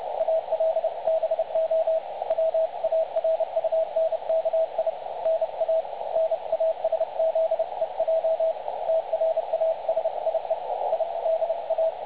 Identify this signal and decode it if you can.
Morse code, heard as a keyed tone, about 30 wpm, 650 Hz, with radio noise. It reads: ALBO WARUNKI NA NASZEJ TRASIE I DA
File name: dlazaawansowanych / nagraniazpasma / websdr recording 3565.0kHz